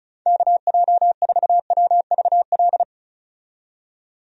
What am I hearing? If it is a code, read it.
Morse code, heard as a keyed tone, about 34 wpm, 700 Hz. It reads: KJ4WVL
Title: USA Call Signs - 6 Characters - 35wpm